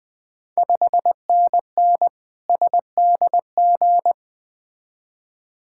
Morse code, heard as a keyed tone, about 20 wpm, 690 Hz. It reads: 5NN SDG
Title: Ontario QSO Party Exchange 20wpm